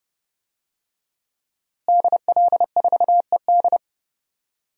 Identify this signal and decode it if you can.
Morse code, heard as a keyed tone, about 31 wpm, 700 Hz. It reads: DL4ED
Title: DXCC-100 Call Signs - 5 Characters - Repeated 2x - 30wpm